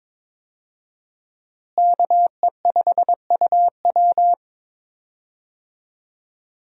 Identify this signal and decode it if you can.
Morse code, heard as a keyed tone, about 22 wpm, 700 Hz. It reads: KE5UW
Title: USA Call Signs - 5 Characters - Repeated 2x - 22wpm